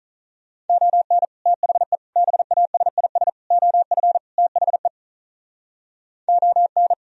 Morse code, heard as a keyed tone, about 44 wpm, 700 Hz. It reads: ON THE BASIS OF THE ON